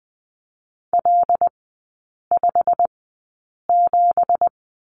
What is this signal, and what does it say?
Morse code, heard as a keyed tone, about 20 wpm, spaced as for 8 wpm, 700 Hz. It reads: L57
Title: 198 Lesson 32e - Letter k - Three Characters 20wpm